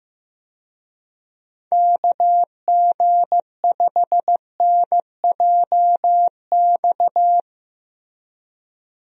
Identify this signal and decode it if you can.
Morse code, heard as a keyed tone, about 15 wpm, 700 Hz. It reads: KG5NJX
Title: USA Call Signs - 6 Characters - Repeated 2x - 15wpm